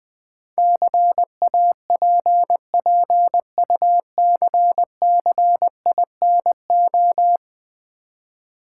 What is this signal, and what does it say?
Morse code, heard as a keyed tone, about 20 wpm, 690 Hz. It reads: CAPPUCCINO
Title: Easy Words with Hints - 20wpm